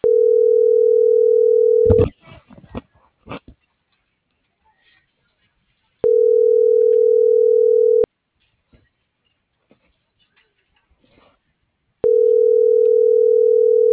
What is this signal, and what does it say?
Ringback tone (North America), 440 + 480 Hz.